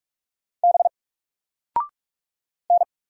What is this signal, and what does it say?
Morse code, heard as a keyed tone, about 44 wpm, 700 Hz. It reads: B N
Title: Single Character (Letter:Number:Punctuation:Prosign) 45wpm